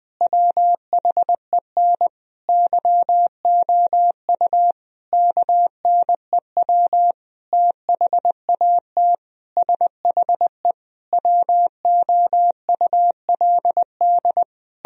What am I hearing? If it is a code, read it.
Morse code, heard as a keyed tone, about 20 wpm, 700 Hz. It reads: WHEN YOU KNEW THAT SHE WOULD
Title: Sentences from Top 1000 Words - Review Part 2 - 20wpm